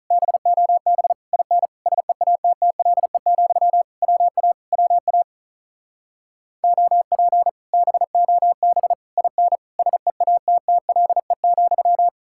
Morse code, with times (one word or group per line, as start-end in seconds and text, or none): 0.1-1.1 BOB
1.3-1.7 IN
1.9-3.8 SEATTLE,
4.0-4.5 WA
4.7-5.2 WA
6.6-7.5 OP
7.7-8.9 BOB
9.2-9.6 IN
9.8-12.1 SEATTLE,